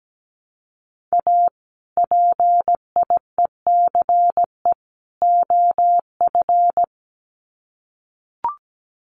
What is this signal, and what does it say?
Morse code, heard as a keyed tone, about 17 wpm, 700 Hz. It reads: A PIECE OF
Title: Sets of 3 Words 17wpm